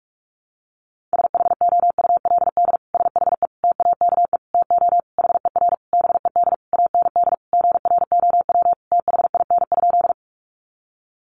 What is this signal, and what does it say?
Morse code, heard as a keyed tone, about 46 wpm, 700 Hz. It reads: SHOULD SHE TAKE TO HER BED AND GROW THIN?
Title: Sentences from Top 1200 Words 45wpm